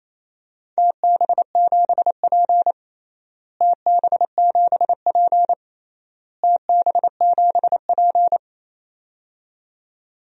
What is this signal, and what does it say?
Morse code, heard as a keyed tone, about 28 wpm, 700 Hz. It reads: TB7P TB7P TB7P
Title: DXCC-100 Call Signs - 4 Characters - Repeated 3x - 28wpm